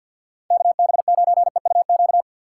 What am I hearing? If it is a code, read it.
Morse code, heard as a keyed tone, about 49 wpm, 700 Hz. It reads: KB9EUQ
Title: USA Call Signs - 4x Character Spacing - 50wpm